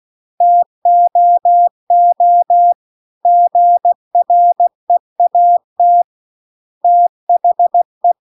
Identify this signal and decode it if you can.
Morse code, heard as a keyed tone, about 16 wpm, 700 Hz. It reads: TOO GREAT THE